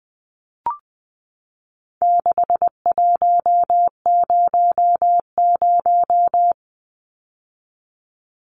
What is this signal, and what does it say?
Morse code, heard as a keyed tone, about 20 wpm, 700 Hz. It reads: 6100
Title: Numbers - 4-Digit 20wpm